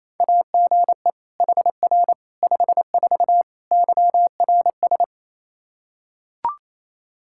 Morse code, heard as a keyed tone, about 28 wpm, 700 Hz. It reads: AGE HR 54 YRS
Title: Practice QSO - Exchange 3 - More Info - 28wpm